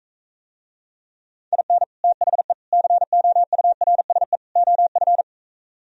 Morse code, heard as a keyed tone, about 44 wpm, 700 Hz. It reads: IN THE COURSE OF